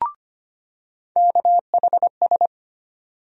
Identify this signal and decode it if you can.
Morse code, heard as a keyed tone, about 25 wpm, 700 Hz. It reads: KHS
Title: USA Call Sign Postfixes - 3 Letters 25wpm